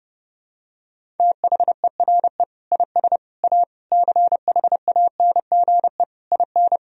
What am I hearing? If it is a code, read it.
Morse code, heard as a keyed tone, about 32 wpm, 700 Hz. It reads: THERE IS A CHANGE IN
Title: Sentences from Top 1000 Words - Review Part 1 - 30wpm